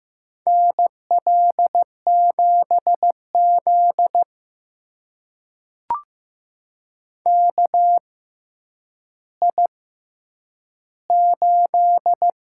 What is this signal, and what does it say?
Morse code, heard as a keyed tone, about 15 wpm, spaced as for 9 wpm, 700 Hz. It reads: NL7Z KI8
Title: USA Call Signs - 4 Characters - 6x Character Spacing - 15wpm